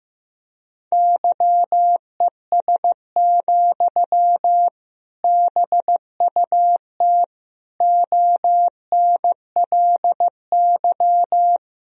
Morse code, heard as a keyed tone, about 15 wpm, 700 Hz. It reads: YES, BUT ONLY
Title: Sentences from Top 500 Words and Abbreviations 15wpm